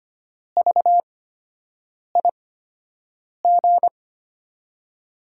Morse code, heard as a keyed tone, about 25 wpm, spaced as for 7 wpm, 700 Hz. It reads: VIG
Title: USA Call Signs - 5 Characters - 8x Character Spacing - 25wpm